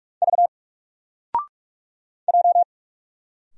Morse code, heard as a keyed tone, about 44 wpm, 700 Hz. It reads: V J
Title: Single Letter-Number 45wpm